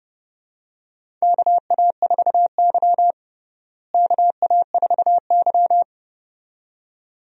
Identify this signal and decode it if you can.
Morse code, heard as a keyed tone, about 29 wpm, 700 Hz. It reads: KA4Y KA4Y
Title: USA Call Signs - 4 Characters - Repeated 2x - 30wpm